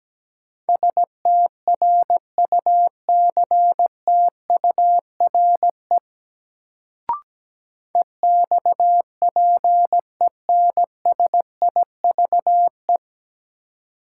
Morse code, heard as a keyed tone, about 17 wpm, 700 Hz. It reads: STRUCTURE EXPENSIVE
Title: Sentences from Top 1500 Words - New Word Drill Alt - 17wpm